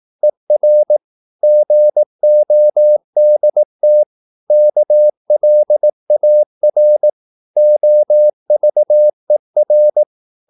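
Morse code, heard as a keyed tone, about 18 wpm, 600 Hz. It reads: ER GODT KLAR OVER
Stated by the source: Slow (QRS) Morse practice at 18 wpm, CW med Gnister.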